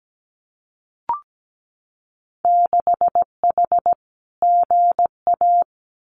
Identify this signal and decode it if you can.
Morse code, heard as a keyed tone, about 17 wpm, 700 Hz. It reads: 6H GA
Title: Winter Field Day Exchanges Encore 17wpm